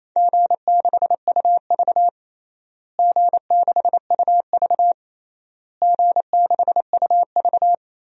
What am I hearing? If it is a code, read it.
Morse code, heard as a keyed tone, about 28 wpm, 700 Hz. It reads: G6UV G6UV G6UV
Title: DXCC-100 Call Signs - 4 Characters - Repeated 3x - 28wpm